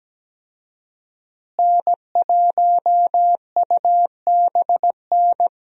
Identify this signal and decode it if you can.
Morse code, heard as a keyed tone, about 17 wpm, 700 Hz. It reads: N1UBN
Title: USA Call Signs - 5 Characters - Repeated 2x - 17wpm